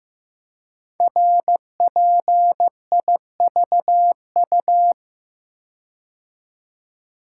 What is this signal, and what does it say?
Morse code, heard as a keyed tone, about 15 wpm, 700 Hz. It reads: RPIVU
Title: Code Groups - 5 Letters - 15wpm